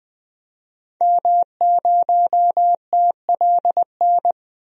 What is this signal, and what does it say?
Morse code, heard as a keyed tone, about 20 wpm, 700 Hz. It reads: M0TLN
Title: DXCC-100 Call Signs - 5 Characters - 20wpm